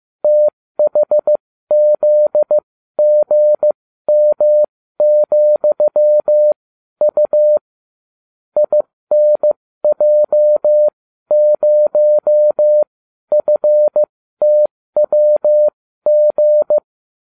Morse code, heard as a keyed tone, about 15 wpm, spaced as for 12 wpm, 600 Hz. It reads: THZGM,U INJ0FTWG